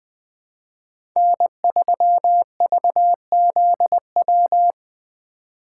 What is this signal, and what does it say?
Morse code, heard as a keyed tone, about 20 wpm, 700 Hz. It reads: N3VZW
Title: USA Call Signs - 5 Characters - 2x Character Spacing - 20wpm